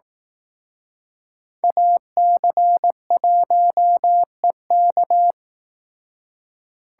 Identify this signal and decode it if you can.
Morse code, heard as a keyed tone, about 18 wpm, 700 Hz. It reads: AC1EK
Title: Speed Racing - USA Call Signs 15wpm